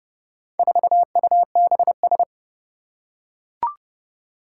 Morse code, heard as a keyed tone, about 30 wpm, 700 Hz. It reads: 4UBS
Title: Four Challenging Letter-Number Combinations 30wpm